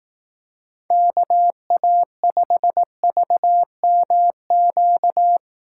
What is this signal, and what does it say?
Morse code, heard as a keyed tone, about 18 wpm, 700 Hz. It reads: KA5VMQ
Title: Speed Racing - USA 2x3 Call Signs 15wpm